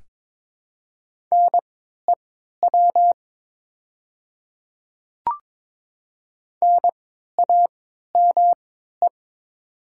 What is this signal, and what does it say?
Morse code, heard as a keyed tone, about 22 wpm, spaced as for 12 wpm, 690 Hz. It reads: NEW NAME
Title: 3x Character Spacing - Top 200 Words - 22wpm